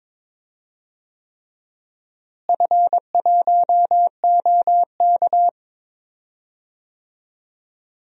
Morse code, heard as a keyed tone, about 22 wpm, 700 Hz. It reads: F1OK
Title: DXCC-100 Call Signs - 4 Characters - Repeated 3x - 22wpm